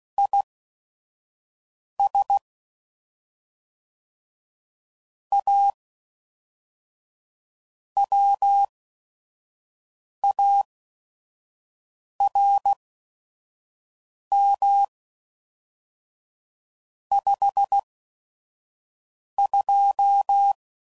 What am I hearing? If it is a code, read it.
Morse code, heard as a keyed tone, about 16 wpm, spaced as for 5 wpm, 800 Hz. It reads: IS A WARM 52
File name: CW Code Groups @5WPM / Weather Sets / Weather 4 10